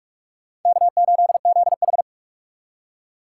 Morse code, heard as a keyed tone, about 44 wpm, 700 Hz. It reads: K8ZH